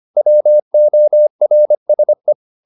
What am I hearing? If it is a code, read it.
Morse code, heard as a keyed tone, about 25 wpm, 600 Hz. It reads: WORSE